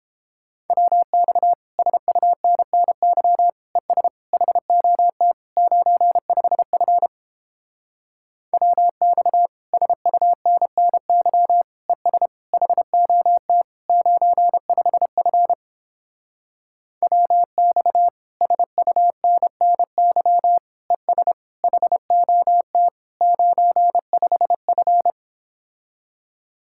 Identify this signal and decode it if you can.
Morse code, heard as a keyed tone, about 29 wpm, 700 Hz. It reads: WX SUNNY ES HOT 95F WX SUNNY ES HOT 95F WX SUNNY ES HOT 95F